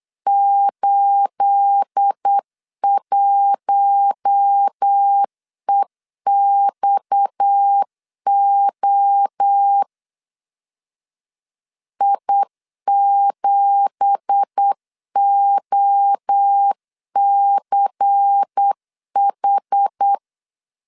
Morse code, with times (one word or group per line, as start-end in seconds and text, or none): 0.3-9.8 81EXO
12.0-20.1 I7OCH